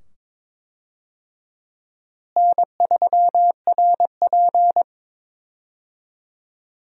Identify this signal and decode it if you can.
Morse code, heard as a keyed tone, about 23 wpm, 700 Hz. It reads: N3RP
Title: USA Call Signs - 4 Characters - Repeated 2x - 22wpm